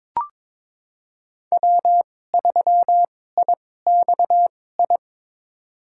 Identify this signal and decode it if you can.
Morse code, heard as a keyed tone, about 22 wpm, spaced as for 16 wpm, 700 Hz. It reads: W3IXI
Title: USA Call Signs - 5 Characters - 2x Character Spacing - 22wpm